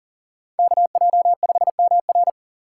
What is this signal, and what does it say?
Morse code, heard as a keyed tone, about 38 wpm, 700 Hz. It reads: KJ5MR
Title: Speed Racing - USA 2x2 Call Signs 40wpm